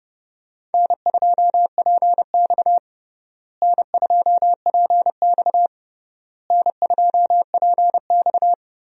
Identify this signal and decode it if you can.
Morse code, heard as a keyed tone, about 32 wpm, 700 Hz. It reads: N2PX N2PX N2PX
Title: USA Call Signs - 4 Characters - Repeated 3x - 30wpm